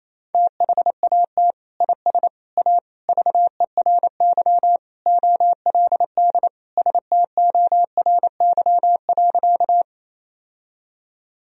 Morse code, heard as a keyed tone, about 28 wpm, 700 Hz. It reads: THAT IS A VERY OLD STORY.
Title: Sentences from Top 1000 Words - Review Part 1 - 28wpm